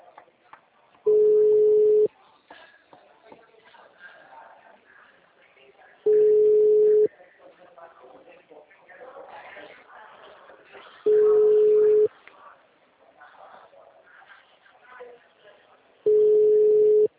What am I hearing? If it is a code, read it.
Ringback tone (Europe, 425 Hz), with speech.
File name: out- -2006- - - .25756